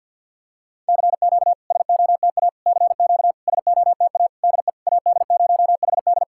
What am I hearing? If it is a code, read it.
Morse code, heard as a keyed tone, about 50 wpm, 700 Hz. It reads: CQ SOTA CQ SOTA DE AD0HD
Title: SOTA QSO Practice 50wpm